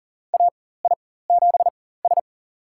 Morse code, heard as a keyed tone, about 40 wpm, spaced as for 19 wpm, 700 Hz. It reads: AI7S
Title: USA Call Signs - 4 Characters - 4x Character Spacing - 40wpm